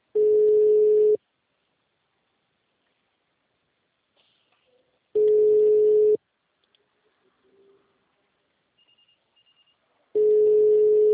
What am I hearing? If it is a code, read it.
Ringback tone (Europe, 425 Hz).